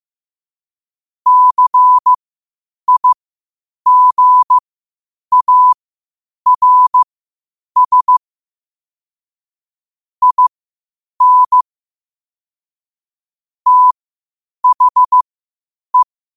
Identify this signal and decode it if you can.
Morse code, heard as a keyed tone, about 15 wpm, spaced as for 8 wpm, 1000 Hz. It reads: CIGARS IN THE
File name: qotdfiles / 8wpm